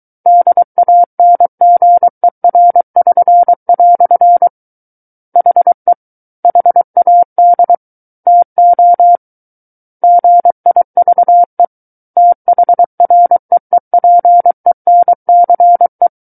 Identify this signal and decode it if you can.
Morse code, heard as a keyed tone, about 23 wpm, 700 Hz. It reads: DANGER<SN>" HE HAD TO GIVE THREEPENCE
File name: Book One / 14-Chapter 14 - 23 WPM